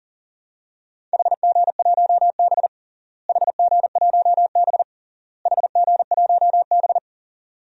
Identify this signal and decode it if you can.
Morse code, heard as a keyed tone, about 37 wpm, 700 Hz. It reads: HG1B HG1B HG1B
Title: DXCC-100 Call Signs - 4 Characters - Repeated 3x - 40wpm